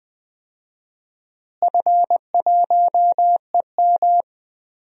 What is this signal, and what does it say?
Morse code, heard as a keyed tone, about 20 wpm, 700 Hz. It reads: F1EM
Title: DXCC-100 Call Signs - 4 Characters - Repeated 3x - 20wpm